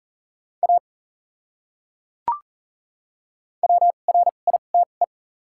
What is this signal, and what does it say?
Morse code, heard as a keyed tone, about 40 wpm, spaced as for 29 wpm, 700 Hz. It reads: A WRITE
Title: 2x Character Spacing - Top 100 Words - 40wpm